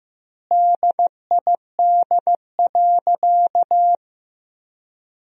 Morse code, heard as a keyed tone, about 15 wpm, 700 Hz. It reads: DID.
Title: 4x Word-Spacing - Sentences from Top 100 Words - 15wpm